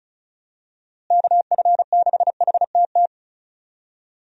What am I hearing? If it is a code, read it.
Morse code, heard as a keyed tone, about 36 wpm, 700 Hz. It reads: KF6HTT
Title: USA Call Signs - 6 Characters - Repeated 3x - 35wpm